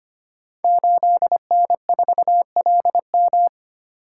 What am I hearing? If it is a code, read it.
Morse code, heard as a keyed tone, about 25 wpm, 700 Hz. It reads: 8N4LM
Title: Code Groups - 5 Letter:Numbers - 25wpm